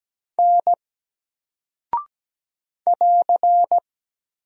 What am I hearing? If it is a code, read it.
Morse code, heard as a keyed tone, about 17 wpm, 700 Hz. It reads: N <AR>
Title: Single Character (Letter:Number:Punctuation:Prosign) Alt 17wpm